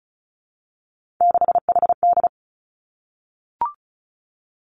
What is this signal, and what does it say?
Morse code, heard as a keyed tone, about 36 wpm, 700 Hz. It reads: 6HD
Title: Three Challenging Letter-Number Combinations 35wpm